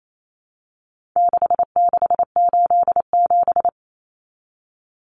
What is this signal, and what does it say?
Morse code, heard as a keyed tone, about 28 wpm, 700 Hz. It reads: 6687
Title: Common Cut Numbers - 4-Digit 28wpm